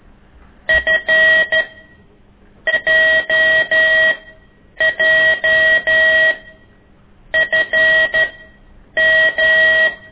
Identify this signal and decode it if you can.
Morse code, heard as a keyed tone, about 12 wpm, spaced as for 7 wpm, 1900 Hz. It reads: FJJFM
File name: ik1weg2 / CORSI CW